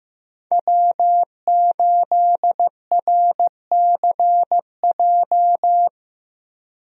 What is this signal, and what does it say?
Morse code, heard as a keyed tone, about 15 wpm, 700 Hz. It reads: W8RCJ
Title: USA Call Signs - 5 Characters - Repeated 3x - 15wpm